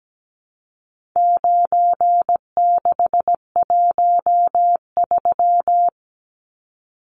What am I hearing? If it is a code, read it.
Morse code, heard as a keyed tone, about 17 wpm, 700 Hz. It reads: 9613
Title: Numbers - 4-Digit 17wpm